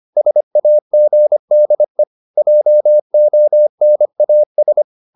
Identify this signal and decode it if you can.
Morse code, heard as a keyed tone, about 25 wpm, 600 Hz. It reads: SAGDE JONAS
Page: Ferie 25 wpm